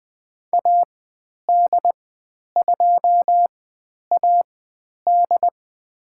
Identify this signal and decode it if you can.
Morse code, heard as a keyed tone, about 21 wpm, spaced as for 10 wpm, 700 Hz. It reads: AD2AD
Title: 109 Lesson 18g - Letter m - Call Signs 20wpm